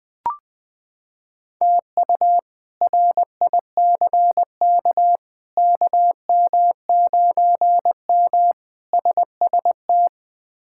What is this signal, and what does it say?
Morse code, heard as a keyed tone, about 21 wpm, 700 Hz. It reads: TU RICK KM9M SST
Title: K1USN SST QSO Practice - 20wpm